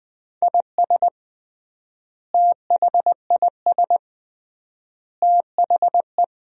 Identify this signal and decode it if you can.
Morse code, heard as a keyed tone, about 20 wpm, 700 Hz. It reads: IS THIS THE
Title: 3x Word-Spacing - Sentences from Top 200 Words - 20wpm